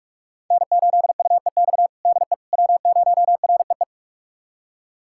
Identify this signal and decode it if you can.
Morse code, heard as a keyed tone, about 41 wpm, 700 Hz. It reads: N8UEX DE W0REE
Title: USA Callsign DE Callsign - 5 Characters - 45wpm